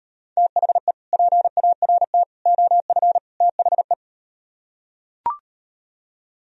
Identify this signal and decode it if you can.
Morse code, heard as a keyed tone, about 38 wpm, 700 Hz. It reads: THE PART OF THE